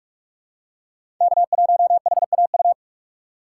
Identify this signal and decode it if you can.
Morse code, heard as a keyed tone, about 44 wpm, 690 Hz. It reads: K1HAU
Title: USA Call Signs - 5 Characters - 4x Character Spacing - 45wpm